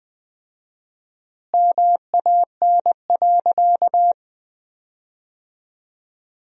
Morse code, heard as a keyed tone, about 20 wpm, 700 Hz. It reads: MAN.
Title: 4x Word-Spacing - Sentences from Top 500 Words - 20wpm